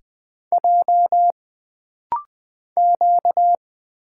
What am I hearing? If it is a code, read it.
Morse code, heard as a keyed tone, about 20 wpm, 700 Hz. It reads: J Q
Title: Single Letters - Rapid-Fire 20wpm